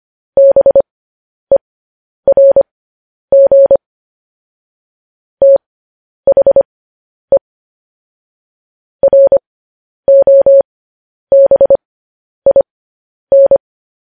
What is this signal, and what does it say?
Morse code, heard as a keyed tone, about 25 wpm, spaced as for 10 wpm, 550 Hz. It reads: BERG THE ROBIN
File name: podcast / News headlines-1x-25x10